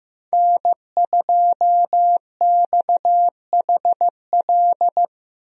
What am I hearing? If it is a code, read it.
Morse code, heard as a keyed tone, about 15 wpm, 700 Hz. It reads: N2XHL
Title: USA Call Signs - 5 Characters - Repeated 2x - 15wpm